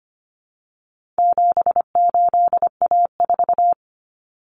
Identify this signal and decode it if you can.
Morse code, heard as a keyed tone, about 25 wpm, 700 Hz. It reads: 78A4
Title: Common Cut Numbers - 4-Digit 25wpm